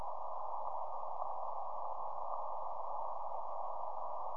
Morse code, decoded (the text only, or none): none